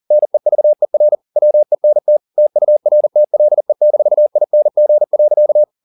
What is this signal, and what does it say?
Morse code, heard as a keyed tone, about 40 wpm, 600 Hz. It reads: NEVER WENT TURTLE-ING.
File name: the old man and the sea 1 40wpm